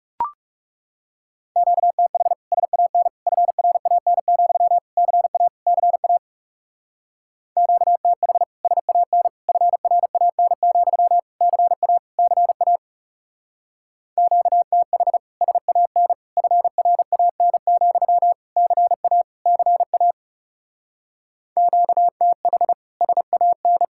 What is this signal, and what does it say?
Morse code, heard as a keyed tone, about 39 wpm, 700 Hz. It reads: QTH SAN FRAN, CA CA QTH SAN FRAN, CA CA QTH SAN FRAN, CA CA QTH SAN